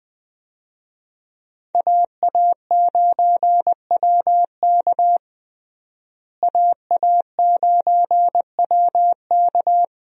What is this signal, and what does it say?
Morse code, heard as a keyed tone, about 21 wpm, 700 Hz. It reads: AA9WK AA9WK
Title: USA Call Signs - 5 Characters - Repeated 3x - 20wpm